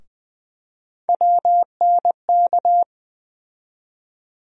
Morse code, heard as a keyed tone, about 20 wpm, 700 Hz. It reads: WNK
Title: USA Call Sign Postfixes - 3 Letters 20wpm